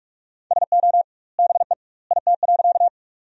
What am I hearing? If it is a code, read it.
Morse code, heard as a keyed tone, about 45 wpm, 700 Hz. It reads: SO BE IT.
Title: 2x Word-Spacing - Sentences from Top 100 Words - 45wpm